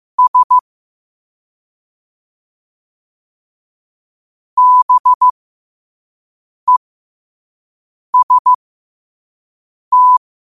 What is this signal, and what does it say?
Morse code, heard as a keyed tone, about 15 wpm, spaced as for 5 wpm, 1000 Hz. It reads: S BEST
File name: qotdfiles / 5wpm